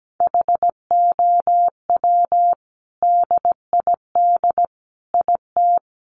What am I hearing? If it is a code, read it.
Morse code, heard as a keyed tone, about 17 wpm, 700 Hz. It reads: HOW DID IT
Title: Sentences from Top 800 Words 17wpm